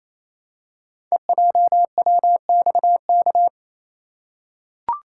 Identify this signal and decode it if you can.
Morse code, heard as a keyed tone, about 28 wpm, 700 Hz. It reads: EJWXK
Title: Code Groups - 5 Letters - 28wpm